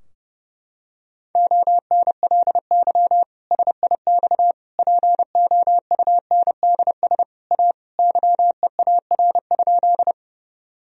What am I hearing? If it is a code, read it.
Morse code, heard as a keyed tone, about 32 wpm, 700 Hz. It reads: ONLY SIX POUNDS A YEAR?
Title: Sentences from Top 1000 Words - Review Part 2 - 30wpm